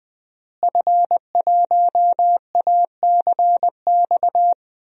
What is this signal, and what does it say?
Morse code, heard as a keyed tone, about 20 wpm, 700 Hz. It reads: F1ACX
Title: DXCC-100 Call Signs - 5 Characters - Repeated 3x - 20wpm